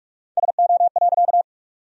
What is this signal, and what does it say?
Morse code, heard as a keyed tone, about 47 wpm, 700 Hz. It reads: SO.
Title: 4x Word-Spacing - Sentences from Top 200 Words - 45wpm